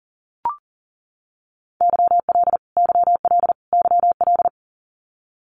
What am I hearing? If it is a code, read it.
Morse code, heard as a keyed tone, about 40 wpm, 700 Hz. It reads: YL YL YL
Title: Abbreviations Repeated 3x - 40wpm